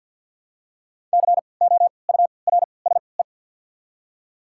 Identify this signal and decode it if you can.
Morse code, heard as a keyed tone, about 48 wpm, spaced as for 28 wpm, 700 Hz. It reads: COURSE
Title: 3x Character Spacing - Top 500 Words - 50wpm